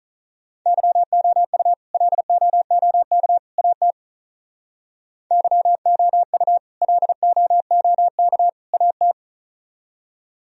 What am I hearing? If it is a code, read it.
Morse code, heard as a keyed tone, about 46 wpm, 700 Hz. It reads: YOU LOOK AT YOU LOOK AT